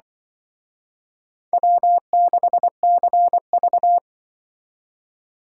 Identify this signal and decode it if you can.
Morse code, heard as a keyed tone, about 24 wpm, 700 Hz. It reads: W6CV
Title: Speed Racing - USA Call Signs 20wpm